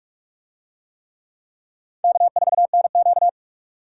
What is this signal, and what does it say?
Morse code, heard as a keyed tone, about 44 wpm, 700 Hz. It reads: K4NQ